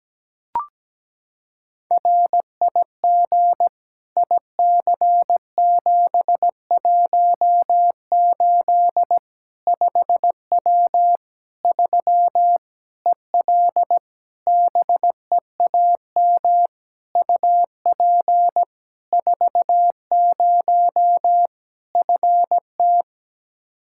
Morse code, heard as a keyed tone, about 17 wpm, 700 Hz. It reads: RIG IC718 5W 3 EL BEAM UP 40 FT
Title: K1USN SST QSO Practice 17wpm